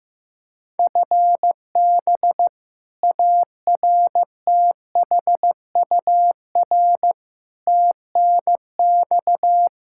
Morse code, heard as a keyed tone, about 15 wpm, 700 Hz. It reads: FB ARTHUR TNX
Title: Practice QSO - Exchange 3 - More Info - 15wpm